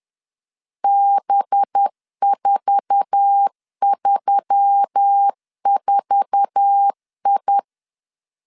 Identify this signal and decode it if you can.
Morse code, heard as a keyed tone, about 10 wpm, 800 Hz. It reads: B434I
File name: LEZIONI / 8 YJ45 / 8 YJ4 10wpm 2633